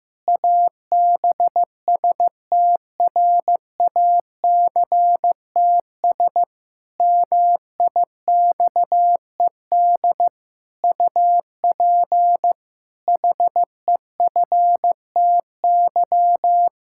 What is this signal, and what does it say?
Morse code, heard as a keyed tone, about 15 wpm, 700 Hz. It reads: ABSTRACTS MIXED UP HEFTY
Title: Pangram Challenge v2 - 15wpm